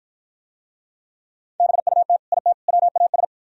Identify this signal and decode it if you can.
Morse code, heard as a keyed tone, about 53 wpm, 700 Hz. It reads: BUT IT WAS